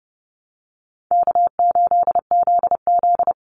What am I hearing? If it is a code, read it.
Morse code, heard as a keyed tone, about 30 wpm, 700 Hz. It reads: K8ZZ
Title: Easy Call Signs Repeated 2x - 30wpm